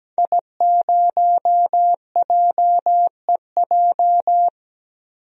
Morse code, heard as a keyed tone, about 17 wpm, 700 Hz. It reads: I0JEJ
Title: DXCC-100 Call Signs - 5 Characters - Repeated 2x - 17wpm